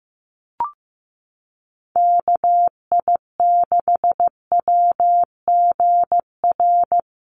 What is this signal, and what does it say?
Morse code, heard as a keyed tone, about 15 wpm, 700 Hz. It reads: KI6WGR
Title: USA Call Signs 15wpm